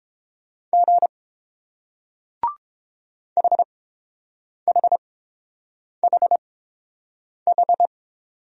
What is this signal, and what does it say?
Morse code, heard as a keyed tone, about 30 wpm, spaced as for 7 wpm, 700 Hz. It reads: G HHHH